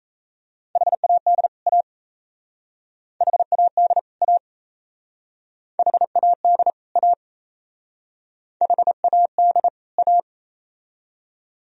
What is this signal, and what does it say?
Morse code, heard as a keyed tone, about 37 wpm, 700 Hz. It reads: HAD A HAD A HAD A HAD A